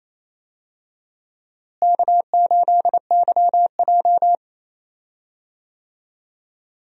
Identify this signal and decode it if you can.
Morse code, heard as a keyed tone, about 28 wpm, 700 Hz. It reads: K8YJ
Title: USA Call Signs - 4 Characters - Repeated 3x - 28wpm